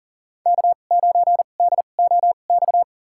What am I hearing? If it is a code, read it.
Morse code, heard as a keyed tone, about 40 wpm, spaced as for 29 wpm, 700 Hz. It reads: K9DOX
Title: USA Call Signs - 2x Character Spacing - 40wpm